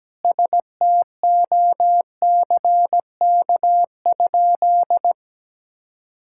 Morse code, heard as a keyed tone, about 17 wpm, 700 Hz. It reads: STOCK?
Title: Sentences from Top 1000 Words - Review Part 5 - 17wpm